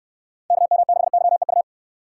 Morse code, heard as a keyed tone, about 68 wpm, 700 Hz. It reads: KM6YEU